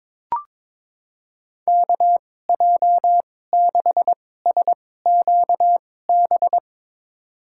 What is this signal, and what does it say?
Morse code, heard as a keyed tone, about 22 wpm, spaced as for 16 wpm, 700 Hz. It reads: KJ6SQB
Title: USA Call Signs - 6 Characters - 2x Character Spacing - 22wpm